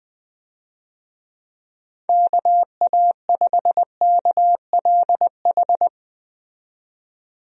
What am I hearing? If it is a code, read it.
Morse code, heard as a keyed tone, about 20 wpm, 700 Hz. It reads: KA5KLH
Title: USA Call Signs - 6 Characters - 20wpm